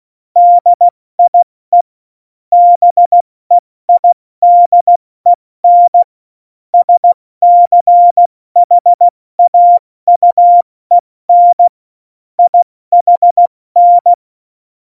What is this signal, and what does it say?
Morse code, heard as a keyed tone, about 16 wpm, 700 Hz. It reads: DIE BEIDEN SCHAUEN IHN
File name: Morsekurs / Witze / Witz 10.Dezember 2019, 70 F80